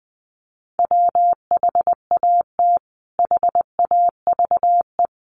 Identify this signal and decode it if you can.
Morse code, heard as a keyed tone, about 19 wpm, 700 Hz. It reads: WHAT HAVE
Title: Sentences from Top 500 Words 20wpm